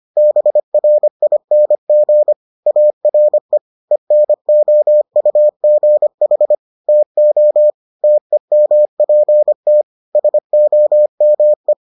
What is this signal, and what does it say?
Morse code, heard as a keyed tone, about 25 wpm, 600 Hz. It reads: BRING ARE ENOUGH TO TEMPT SOME